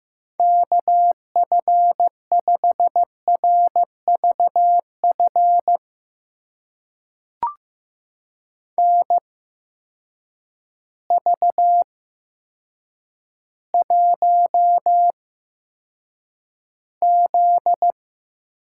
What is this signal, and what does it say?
Morse code, heard as a keyed tone, about 15 wpm, spaced as for 8 wpm, 700 Hz. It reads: KF5RVF NV1Z
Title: USA Call Signs - 8x Character Spacing - 15wpm